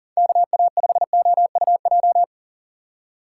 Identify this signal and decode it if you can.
Morse code, heard as a keyed tone, about 40 wpm, 700 Hz. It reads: KA5OUJ
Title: USA Call Signs - 6 Characters - 3x Character Spacing - 40wpm